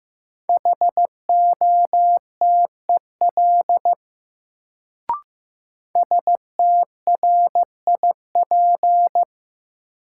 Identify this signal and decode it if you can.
Morse code, heard as a keyed tone, about 15 wpm, 700 Hz. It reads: HOTEL STRIP
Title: Sentences from Top 1600 Words - New Word Drill Alt - 15wpm